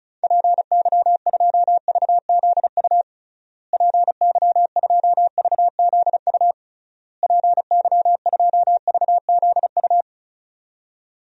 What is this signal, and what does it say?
Morse code, heard as a keyed tone, about 34 wpm, 700 Hz. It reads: PY2VZU PY2VZU PY2VZU
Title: DXCC-100 Call Signs - 6 Characters - Repeated 3x - 35wpm